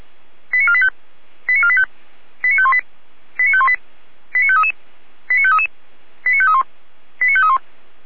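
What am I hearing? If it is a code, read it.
Five-tone selective call (ZVEI), heard as quick runs of five tones, tones of about 70 ms. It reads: 89485 89485 89518 89518 89533 89533 89531 89531
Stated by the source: ZVEI five-tone call 89417 received with an SDR.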